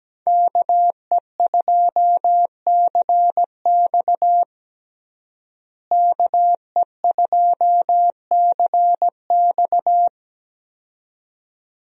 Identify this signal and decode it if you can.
Morse code, heard as a keyed tone, about 17 wpm, 700 Hz. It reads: KE2CX KE2CX
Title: USA Call Signs - 5 Characters - Repeated 2x - 17wpm